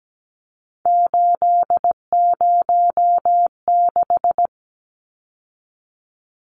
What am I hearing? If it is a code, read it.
Morse code, heard as a keyed tone, about 17 wpm, 700 Hz. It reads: 806
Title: Numbers - 3-Digit 17wpm